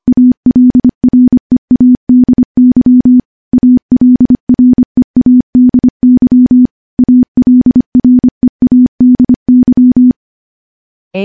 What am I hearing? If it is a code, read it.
Morse code, heard as a keyed tone, about 25 wpm, 250 Hz. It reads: ALREADY ALREADY ALREADY
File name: morse / morse-training-25WPM